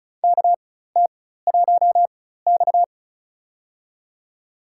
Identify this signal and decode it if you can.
Morse code, heard as a keyed tone, about 36 wpm, spaced as for 16 wpm, 700 Hz. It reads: KT1X
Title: USA Call Signs - 4x Character Spacing - 35wpm